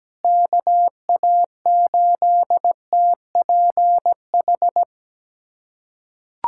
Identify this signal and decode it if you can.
Morse code, heard as a keyed tone, about 17 wpm, 700 Hz. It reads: KA8TPH
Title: USA Callsign DE Callsign - 6 Characters - 17wpm